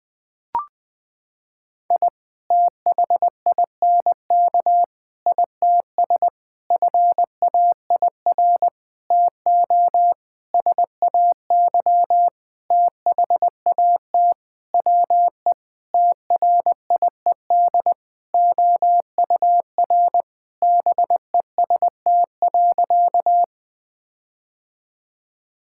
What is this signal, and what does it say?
Morse code, heard as a keyed tone, about 20 wpm, 700 Hz. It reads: I THINK ITS FAIR TO SAY THAT WE TRIED OUR BEST.
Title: Sentences with 7-Word Phrases - 20wpm